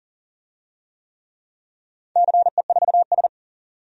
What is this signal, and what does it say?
Morse code, heard as a keyed tone, about 40 wpm, 700 Hz. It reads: CE4S
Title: DXCC-100 Call Signs - 4 Characters - 40wpm